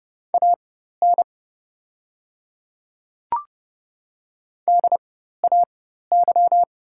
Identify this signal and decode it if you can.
Morse code, heard as a keyed tone, about 30 wpm, spaced as for 14 wpm, 700 Hz. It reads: AN DAY
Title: 4x Character Spacing - Top 100 Words - 30wpm